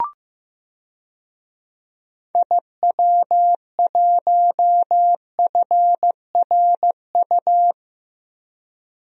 Morse code, heard as a keyed tone, about 15 wpm, 700 Hz. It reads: IW1FRU
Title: DXCC-100 Call Signs - 6 Characters - 15wpm